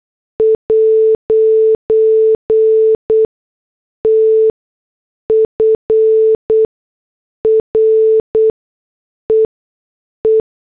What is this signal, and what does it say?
Morse code, heard as a keyed tone, about 8 wpm, spaced as for 6 wpm, 440 Hz. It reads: 'T FREE